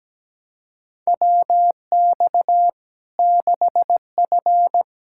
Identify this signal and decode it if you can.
Morse code, heard as a keyed tone, about 17 wpm, 700 Hz. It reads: WX 6F
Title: Practice QSO - Exchange 2 - Info - 17wpm